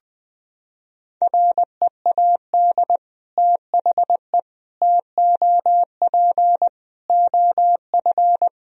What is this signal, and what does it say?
Morse code, heard as a keyed tone, about 20 wpm, 700 Hz. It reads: READ THE TOP OF
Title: Sentences from Top 1000 Words - Review Part 2 - 20wpm